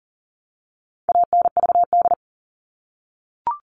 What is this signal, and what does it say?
Morse code, heard as a keyed tone, about 40 wpm, 700 Hz. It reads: ANVD
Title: All Cut Numbers - 4-Digit 40wpm